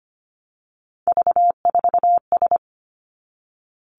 Morse code, heard as a keyed tone, about 25 wpm, 700 Hz. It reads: V4S
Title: Three Challenging Letter-Number Combinations 25wpm